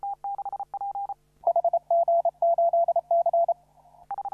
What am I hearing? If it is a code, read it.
Morse code, heard as a keyed tone, about 30 wpm, 680 Hz. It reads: HG8C
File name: CW / RHOB CW 2017 HA6P 0001